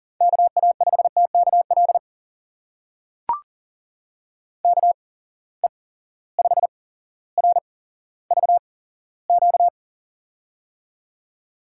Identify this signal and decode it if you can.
Morse code, heard as a keyed tone, about 39 wpm, spaced as for 17 wpm, 700 Hz. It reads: KA5TKL KE5RVQ
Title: USA Call Signs - 6 Characters - 8x Character Spacing - 40wpm